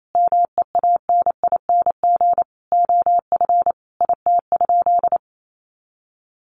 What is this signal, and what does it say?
Morse code, heard as a keyed tone, about 28 wpm, 700 Hz. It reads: MEANING OF IT?
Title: Sentences from Top 900 Words 28wpm